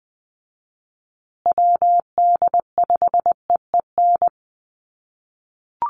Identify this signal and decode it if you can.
Morse code, heard as a keyed tone, about 20 wpm, 700 Hz. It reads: WD5EEN
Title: USA Call Signs Repeated 3x 20wpm